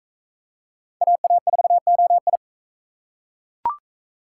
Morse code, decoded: AA4OI

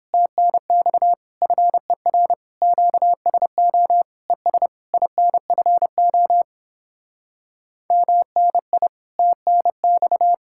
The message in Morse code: TNX FER QSO ES INFO MNI TNX

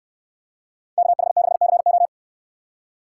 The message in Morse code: KH6CO

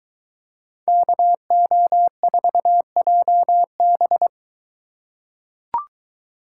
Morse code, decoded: KO4JB

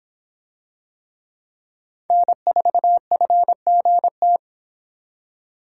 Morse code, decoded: N4FGT

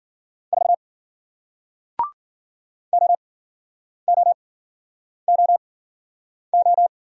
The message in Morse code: 4 OOOO

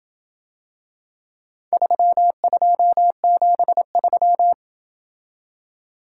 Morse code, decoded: 3273

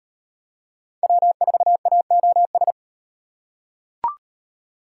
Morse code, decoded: W4AOS

